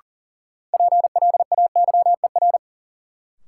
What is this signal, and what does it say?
Morse code, heard as a keyed tone, about 37 wpm, 700 Hz. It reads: PLAYER
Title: Sentences from Top 1000 Words - Review Part 4 - New Word Drill - 40wpm